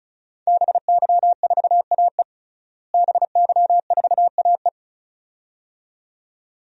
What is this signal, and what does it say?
Morse code, heard as a keyed tone, about 34 wpm, 700 Hz. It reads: BY4AE BY4AE
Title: DXCC-100 Call Signs - 5 Characters - Repeated 2x - 35wpm